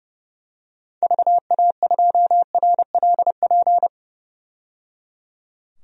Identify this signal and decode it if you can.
Morse code, heard as a keyed tone, about 30 wpm, 700 Hz. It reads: VA2RLP
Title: DXCC-100 Call Signs - 6 Characters - 30wpm